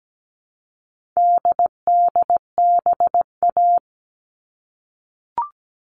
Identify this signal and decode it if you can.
Morse code, heard as a keyed tone, about 17 wpm, 700 Hz. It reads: DDBA
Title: All Cut Numbers - 4-Digit 17wpm